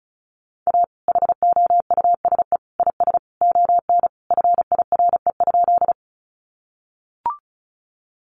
Morse code, heard as a keyed tone, about 34 wpm, 700 Hz. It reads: A HOUSE IS ON FIRE?
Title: Sentences from Top 400 Words 35wpm